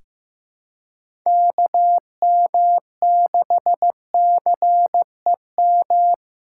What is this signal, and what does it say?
Morse code, heard as a keyed tone, about 15 wpm, 700 Hz. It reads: KM6CEM
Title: USA Callsign DE Callsign - 6 Characters - 15wpm